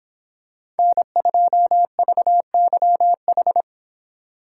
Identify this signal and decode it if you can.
Morse code, heard as a keyed tone, about 26 wpm, 700 Hz. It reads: N2VYH